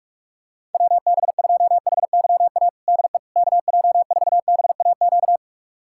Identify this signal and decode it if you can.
Morse code, heard as a keyed tone, about 46 wpm, 700 Hz. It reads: WB2HYA DE KJ4BAQ
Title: USA Callsign DE Callsign - 6 Characters - 45wpm